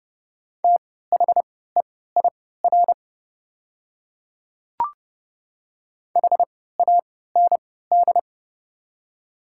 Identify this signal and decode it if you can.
Morse code, heard as a keyed tone, about 30 wpm, spaced as for 17 wpm, 700 Hz. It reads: THEIR HAND
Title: 3x Character Spacing - Top 300 Words - 30wpm